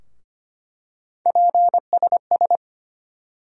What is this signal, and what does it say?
Morse code, heard as a keyed tone, about 25 wpm, 700 Hz. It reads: PSS
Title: USA Call Sign Postfixes - 3 Letters 25wpm